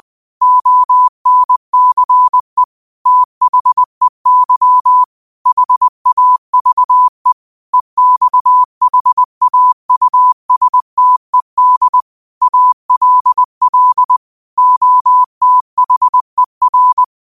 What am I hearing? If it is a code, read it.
Morse code, heard as a keyed tone, about 20 wpm, 1000 Hz. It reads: ONCE THEY HAVE EXHAUSTED ALL OTHER